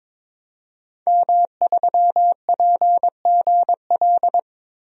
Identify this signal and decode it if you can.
Morse code, heard as a keyed tone, about 22 wpm, 700 Hz. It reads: M3PGL
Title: DXCC-100 Call Signs - 5 Characters - Repeated 2x - 22wpm